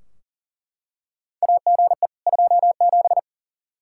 Morse code, heard as a keyed tone, about 40 wpm, 700 Hz. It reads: AGE 27
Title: K1USN SST QSO Practice 40wpm